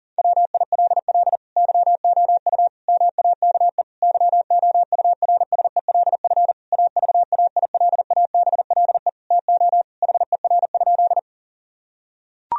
Morse code, heard as a keyed tone, about 37 wpm, 700 Hz. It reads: WILL YOU MAKE YOURSELF AVAILABLE TO HER?
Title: Sentences from Top 1000 Words - Review Part 3 - 40wpm